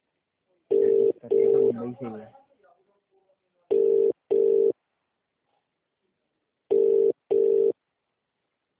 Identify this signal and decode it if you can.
Ringback tone (UK), 400 + 450 Hz, with speech.